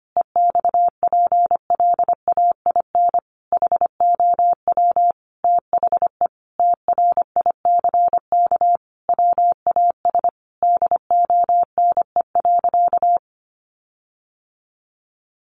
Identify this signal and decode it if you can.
Morse code, heard as a keyed tone, about 25 wpm, 700 Hz. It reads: EXPLAIN HOW THE TRICK WAS DONE.
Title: Sentences from Top 1500 Words 25wpm